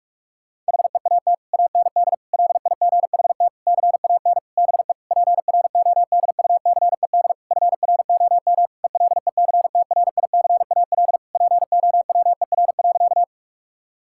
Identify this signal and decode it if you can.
Morse code, heard as a keyed tone, about 46 wpm, 700 Hz. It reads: HEAT AND LIGHT CAN BE PRODUCED FROM ELECTRICAL POWER.
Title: Sentences from Top 1000 Words - Review Part 6 - 45wpm